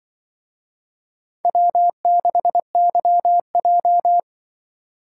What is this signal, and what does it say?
Morse code, heard as a keyed tone, about 28 wpm, 700 Hz. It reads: W6YJ